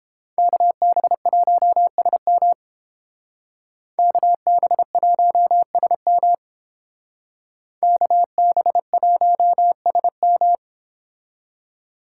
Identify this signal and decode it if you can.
Morse code, heard as a keyed tone, about 29 wpm, 700 Hz. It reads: KB1SM KB1SM KB1SM